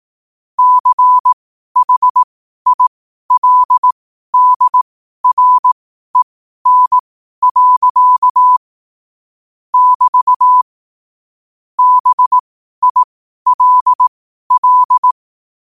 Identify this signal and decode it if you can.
Morse code, heard as a keyed tone, about 18 wpm, spaced as for 12 wpm, 1000 Hz. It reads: CHILDREN. <BT> BILL
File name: qotdfiles / 12wpm